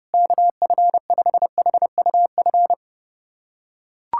Morse code, heard as a keyed tone, about 30 wpm, 700 Hz. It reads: KF5HUF
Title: USA Callsign DE Callsign 30wpm